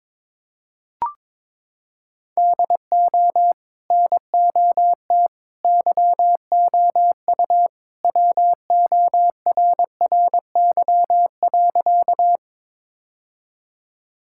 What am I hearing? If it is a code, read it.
Morse code, heard as a keyed tone, about 22 wpm, 700 Hz. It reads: DO NOT YOU WORRY.
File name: practice / 22wpm / Sentences with 3-Word Phrases - 22wpm